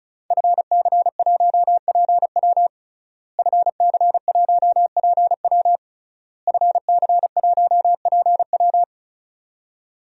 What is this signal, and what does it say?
Morse code, heard as a keyed tone, about 36 wpm, 700 Hz. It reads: FC1PW FC1PW FC1PW
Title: DXCC-100 Call Signs - 5 Characters - Repeated 3x - 35wpm